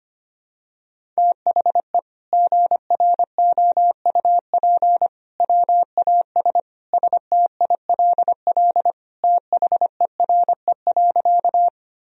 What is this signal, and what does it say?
Morse code, heard as a keyed tone, about 25 wpm, 700 Hz. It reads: THE GROUP WAS STILL THERE.
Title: Sentences from Top 1000 Words - Review Part 1 - 25wpm